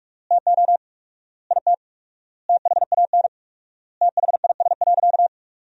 Morse code, heard as a keyed tone, about 46 wpm, 700 Hz. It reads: TO IT THAN THIS.
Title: 4x Word-Spacing - Sentences from Top 100 Words - 45wpm